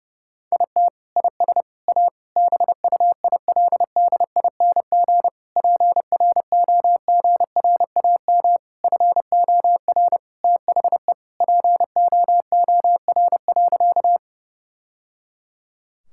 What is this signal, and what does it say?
Morse code, heard as a keyed tone, about 30 wpm, 700 Hz. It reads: IT IS A BUILDING PROGRAM FOR THE POOR.
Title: Sentences from Top 1000 Words - Review Part 5 - 30wpm